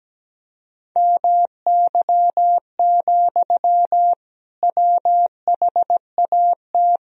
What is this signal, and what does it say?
Morse code, heard as a keyed tone, about 17 wpm, 700 Hz. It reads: MY, WHAT
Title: Sentences from Top 1000 Words - Review Part 2 - 17wpm